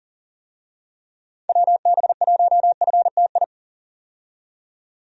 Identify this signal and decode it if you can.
Morse code, heard as a keyed tone, about 40 wpm, 700 Hz. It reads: WB1FTI
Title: USA Call Signs - 6 Characters - 40wpm